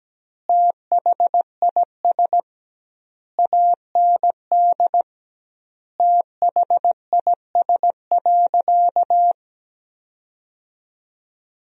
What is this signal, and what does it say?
Morse code, heard as a keyed tone, about 17 wpm, 700 Hz. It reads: THIS AND THIS.
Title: 2x Word-Spacing - Sentences from Top 200 Words - 17wpm